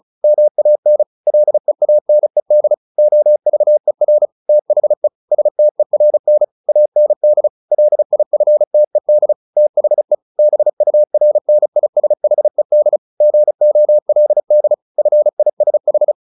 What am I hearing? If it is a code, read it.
Morse code, heard as a keyed tone, about 34 wpm, 600 Hz. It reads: MAN LEANED OVER THE STERN AND LIFTED THE BURNISHED GOLD FISH
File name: the old man and the sea 2 35wpm